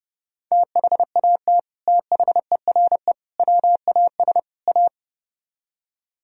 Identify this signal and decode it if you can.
Morse code, heard as a keyed tone, about 37 wpm, 700 Hz. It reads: THAT THERE WAS A
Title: Speed-Racing - Sets of 4 Words 30wpm